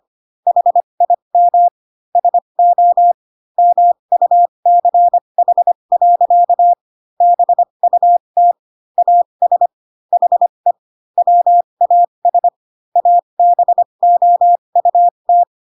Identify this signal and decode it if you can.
Morse code, heard as a keyed tone, about 25 wpm, spaced as for 22 wpm, 700 Hz. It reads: HIM SO MUCH. BUT AS HE WAS ABOUT
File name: Pinocchio / Pinocchio ch2 18 25 ik1hge